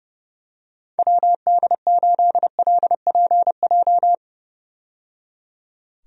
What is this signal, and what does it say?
Morse code, heard as a keyed tone, about 30 wpm, 700 Hz. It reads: WD8LPJ
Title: USA Call Signs - 6 Characters - 30wpm